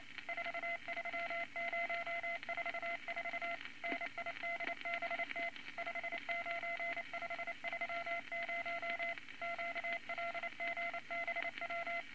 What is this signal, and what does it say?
Morse code, heard as a keyed tone, about 28 wpm, 2050 Hz, with noise. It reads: 43044 NIDBT 59530 QLGBW
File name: Morse Code Records / 22 A LCM 172 Twenty Words Per Minute